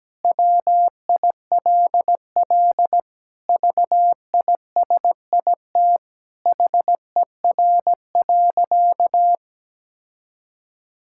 Morse code, heard as a keyed tone, about 17 wpm, 700 Hz. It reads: WILL VISIT HER.
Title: Sentences from Top 1000 Words - Review Part 6 - 17wpm